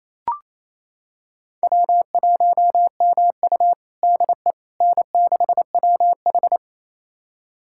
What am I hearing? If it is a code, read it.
Morse code, heard as a keyed tone, about 28 wpm, 700 Hz. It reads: W1MU DE N6WH
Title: USA Callsign DE Callsign - 4 Characters - 28wpm